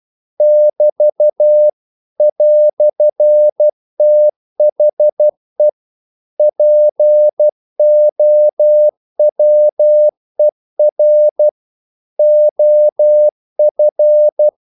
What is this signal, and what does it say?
Morse code, heard as a keyed tone, about 12 wpm, 600 Hz. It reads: <BT> "THE POWER OF
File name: CW / Practice / EnglishQuotations0112